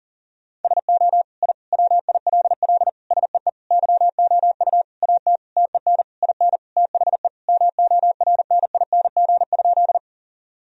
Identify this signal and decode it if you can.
Morse code, heard as a keyed tone, about 40 wpm, 700 Hz. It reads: SO I WILL SEE YOU AT TEN IN THE MORNING?
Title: Sentences from Top 1000 Words - Review Part 2 - 40wpm